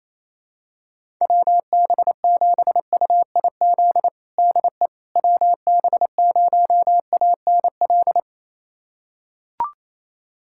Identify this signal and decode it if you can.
Morse code, heard as a keyed tone, about 28 wpm, 700 Hz. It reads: WB7UIZ DE WB0ANL
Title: USA Callsign DE Callsign - 6 Characters - 28wpm